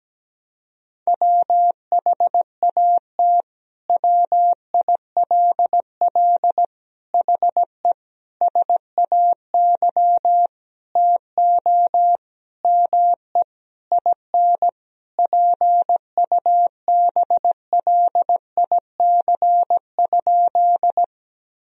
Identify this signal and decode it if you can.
Morse code, heard as a keyed tone, about 17 wpm, 700 Hz. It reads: WHAT WILL HE SAY TO ME IN PUBLIC?
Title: Sentences from Top 1000 Words - Review Part 3 - 17wpm